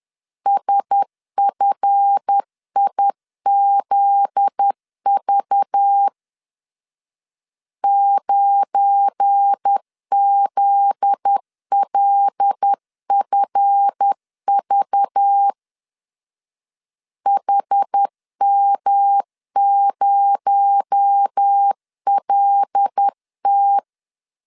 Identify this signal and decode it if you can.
Morse code, heard as a keyed tone, about 10 wpm, 800 Hz. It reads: SFIZV 9ZLFV HM0LT
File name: LEZIONI / 7 GF9Z / 7 GF9Z 10wpm 2611